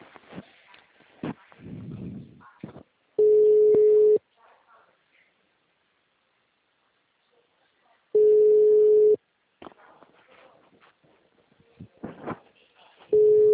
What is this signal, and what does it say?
Ringback tone (Europe, 425 Hz), with speech.
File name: out- -2008- - - .4